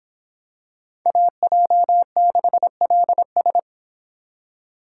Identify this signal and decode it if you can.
Morse code, heard as a keyed tone, about 24 wpm, 700 Hz. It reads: AJ6LS